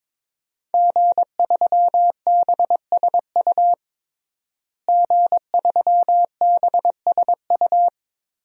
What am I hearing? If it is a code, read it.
Morse code, heard as a keyed tone, about 22 wpm, 700 Hz. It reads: G3BSU G3BSU
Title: DXCC-100 Call Signs - 5 Characters - Repeated 3x - 22wpm